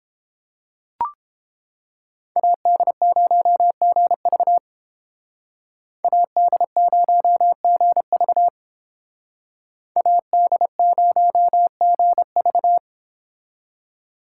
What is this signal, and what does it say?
Morse code, heard as a keyed tone, about 30 wpm, 700 Hz. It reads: AD0GV AD0GV AD0GV